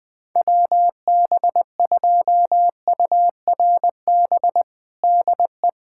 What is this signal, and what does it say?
Morse code, heard as a keyed tone, about 20 wpm, 700 Hz. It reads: WB2URB DE
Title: USA Callsign DE Callsign 20wpm